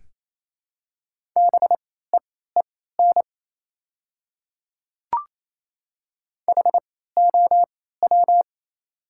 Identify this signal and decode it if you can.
Morse code, heard as a keyed tone, about 28 wpm, spaced as for 16 wpm, 700 Hz. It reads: BEEN HOW
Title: 3x Character Spacing - Top 100 Words - 28wpm